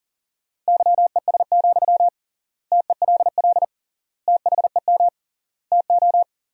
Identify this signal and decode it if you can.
Morse code, heard as a keyed tone, about 37 wpm, 700 Hz. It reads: YES, TELL THEM TO
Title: 3x Word-Spacing - Sentences from Top 500 Words - 40wpm